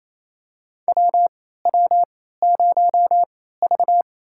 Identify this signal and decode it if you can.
Morse code, heard as a keyed tone, about 28 wpm, spaced as for 16 wpm, 700 Hz. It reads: WW0V
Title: USA Call Signs - 4 Characters - 3x Character Spacing - 28wpm